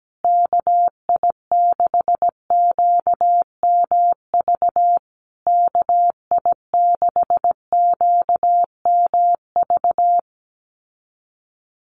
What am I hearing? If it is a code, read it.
Morse code, heard as a keyed tone, about 17 wpm, 700 Hz. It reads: KI6QMV KI6QMV
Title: USA Call Signs Repeated 3x 17wpm